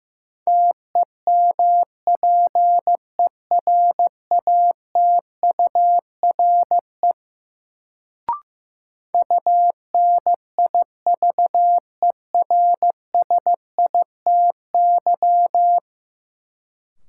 Morse code, heard as a keyed tone, about 15 wpm, 700 Hz. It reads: TEMPERATURE UNIVERSITY
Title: Sentences from Top 1000 Words - Review Part 4 - New Word Drill - 15wpm